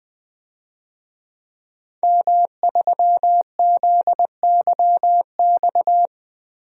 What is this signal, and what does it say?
Morse code, heard as a keyed tone, about 20 wpm, 700 Hz. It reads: M3ZYX
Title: DXCC-100 Call Signs - 5 Characters - 20wpm